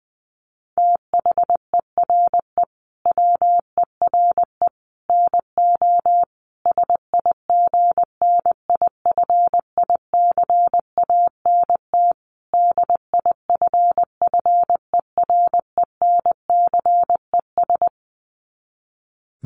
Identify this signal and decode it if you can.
Morse code, heard as a keyed tone, about 20 wpm, 700 Hz. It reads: THERE WERE NO SIGNIFICANT DIFFERENCES
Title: Sets of 5 Words Encore2 20wpm